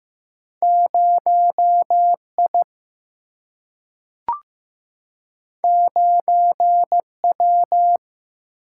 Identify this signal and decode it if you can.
Morse code, heard as a keyed tone, about 15 wpm, 700 Hz. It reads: 0I 9W
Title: Single-Letter Single-Number 15wpm